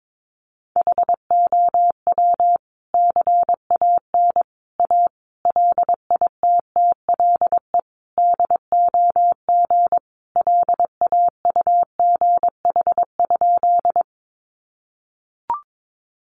Morse code, heard as a keyed tone, about 22 wpm, 700 Hz. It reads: HOW CAN A LITTLE DOG LAUGH?
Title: Sentences from Top 500 Words 22wpm